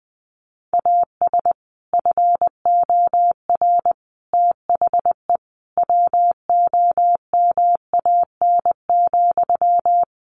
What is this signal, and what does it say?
Morse code, heard as a keyed tone, about 20 wpm, 700 Hz. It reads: AS FOR THE WOMAN,
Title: Sentences from Top 1100 Words 20wpm